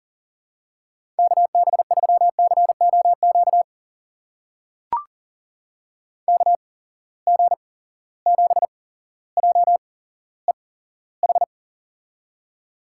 Morse code, heard as a keyed tone, about 41 wpm, spaced as for 17 wpm, 700 Hz. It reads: KB3COQ KG7JEH
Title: USA Call Signs - 6 Characters - 8x Character Spacing - 40wpm